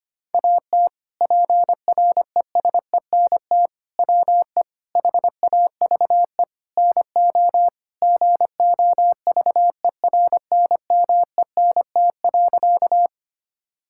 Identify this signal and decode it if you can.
Morse code, heard as a keyed tone, about 25 wpm, 700 Hz. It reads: AT PRESENT WE HAVE NO GOVERNMENT.
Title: Sentences from Top 1000 Words - Review Part 5 - 25wpm